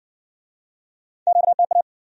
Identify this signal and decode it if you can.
Morse code, heard as a keyed tone, about 57 wpm, 700 Hz. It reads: QTA